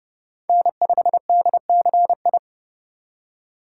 Morse code, heard as a keyed tone, about 39 wpm, 700 Hz. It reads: N5DCI